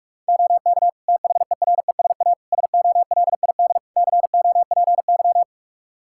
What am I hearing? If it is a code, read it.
Morse code, heard as a keyed tone, about 42 wpm, 700 Hz. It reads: OK THERESA SOLID COPY
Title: Practice QSO - Exchange 3 - More Info - 45wpm